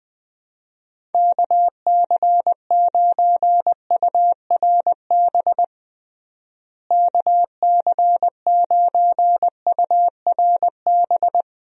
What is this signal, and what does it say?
Morse code, heard as a keyed tone, about 20 wpm, 700 Hz. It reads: KC9URB KC9URB
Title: USA Call Signs - 6 Characters - Repeated 2x - 20wpm